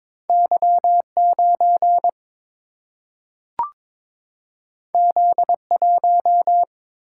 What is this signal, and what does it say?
Morse code, heard as a keyed tone, about 22 wpm, 700 Hz. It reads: Y 9 Z 1
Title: Single-Letter Single-Number 22wpm